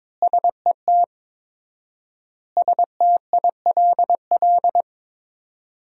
Morse code, heard as a keyed tone, about 22 wpm, 700 Hz. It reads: SET STILL
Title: 4x Word-Spacing - Sentences from Top 300 Words - 22wpm